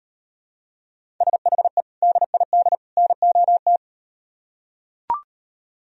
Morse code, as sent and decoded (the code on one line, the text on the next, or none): ... .... / -.. .. -.. / -. --- -
SHE DID NOT